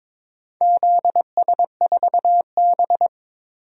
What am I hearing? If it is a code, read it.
Morse code, heard as a keyed tone, about 22 wpm, 700 Hz. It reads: ZS4B
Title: DXCC-100 Call Signs - 4 Characters - 22wpm